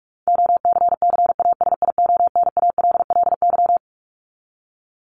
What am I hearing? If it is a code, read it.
Morse code, heard as a keyed tone, about 44 wpm, 700 Hz. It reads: OCCASIONALLY
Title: Top 1001-1250 Words 45wpm